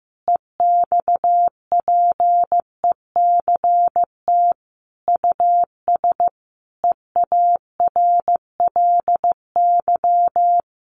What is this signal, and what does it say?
Morse code, heard as a keyed tone, about 15 wpm, 700 Hz. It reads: EXPECT US EARLY
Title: Sentences from Top 1500 Words 15wpm